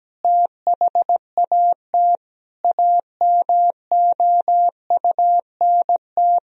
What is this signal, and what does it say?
Morse code, heard as a keyed tone, about 17 wpm, 700 Hz. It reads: THAT AMOUNT
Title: Sentences from Top 1000 Words - Review Part 4 - 17wpm